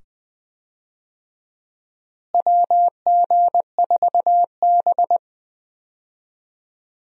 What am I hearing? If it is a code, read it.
Morse code, heard as a keyed tone, about 20 wpm, 700 Hz. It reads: WG4B
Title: USA Call Signs - 4 Characters - Repeated 2x - 20wpm